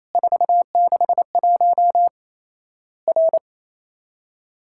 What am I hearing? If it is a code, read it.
Morse code, heard as a keyed tone, about 28 wpm, 700 Hz. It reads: 461 R
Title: Worked All Europe DX Contest - QTC Exchange - 28wpm